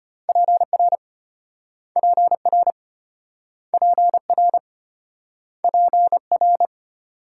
Morse code, heard as a keyed tone, about 33 wpm, 700 Hz. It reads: PR PR PR PR